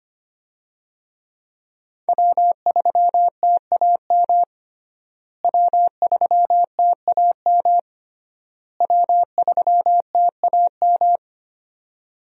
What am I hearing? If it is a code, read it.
Morse code, heard as a keyed tone, about 25 wpm, 700 Hz. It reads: W3TAM W3TAM W3TAM
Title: USA Call Signs - 5 Characters - Repeated 3x - 25wpm